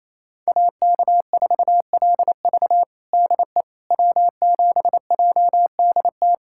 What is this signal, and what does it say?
Morse code, heard as a keyed tone, about 28 wpm, 700 Hz. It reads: AK4LV DE W7JDT
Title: USA Callsign DE Callsign - 5 Characters - 28wpm